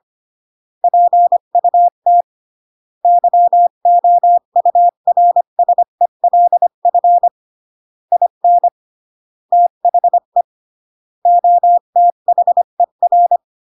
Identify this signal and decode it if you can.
Morse code, heard as a keyed tone, about 25 wpm, 700 Hz. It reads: PUT YOURSELF IN THE OTHER
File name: CW / jodorowsky / 82 commandments 25-20